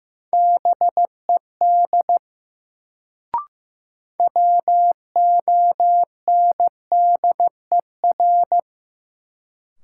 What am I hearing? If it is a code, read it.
Morse code, heard as a keyed tone, about 15 wpm, 700 Hz. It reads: BED WONDER
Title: Sentences from Top 500 Words - New Word Drill - 15wpm